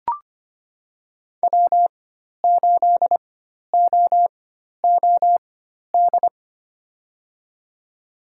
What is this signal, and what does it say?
Morse code, heard as a keyed tone, about 25 wpm, spaced as for 12 wpm, 700 Hz. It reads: W8OOD
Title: USA Call Signs - 5 Characters - 4x Character Spacing - 25wpm